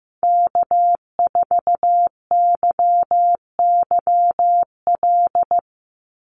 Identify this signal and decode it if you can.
Morse code, heard as a keyed tone, about 15 wpm, 700 Hz. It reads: K4YYL
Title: Call Signs Repeated 2x - 15wpm